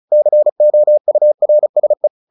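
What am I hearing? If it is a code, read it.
Morse code, heard as a keyed tone, about 34 wpm, 600 Hz. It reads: COURSE
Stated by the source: Morse practice at 35 wpm from CW med Gnister, sending Ken Follett's novel Never in English.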